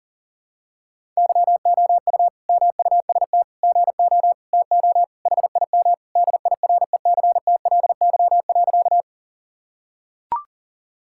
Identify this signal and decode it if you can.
Morse code, heard as a keyed tone, about 40 wpm, 700 Hz. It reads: YOU MUST GO TO HIM DIRECTLY.
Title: Sentences from Top 1000 Words - Review Part 5 - 40wpm